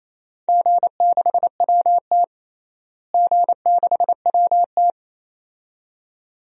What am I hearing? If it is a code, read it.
Morse code, heard as a keyed tone, about 28 wpm, 700 Hz. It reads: G6WT G6WT
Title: DXCC-100 Call Signs - 4 Characters - Repeated 2x - 28wpm